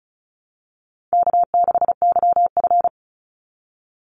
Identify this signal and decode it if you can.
Morse code, heard as a keyed tone, about 34 wpm, 700 Hz. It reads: K6YF
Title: Easy Call Signs Repeated 3x - 35wpm